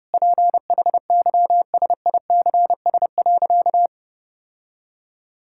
Morse code, heard as a keyed tone, about 30 wpm, 700 Hz. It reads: PHYSICS.
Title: Sentences from Top 1000 Words - Review Part 5 - 30wpm